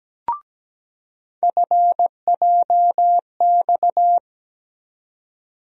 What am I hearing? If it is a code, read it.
Morse code, heard as a keyed tone, about 17 wpm, 700 Hz. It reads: FJX
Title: USA Call Sign Postfixes - 3 Letters 17wpm